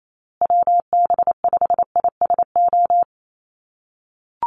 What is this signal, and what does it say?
Morse code, heard as a keyed tone, about 28 wpm, 700 Hz. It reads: WB5ISO
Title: USA Call Signs Repeated 3x 28wpm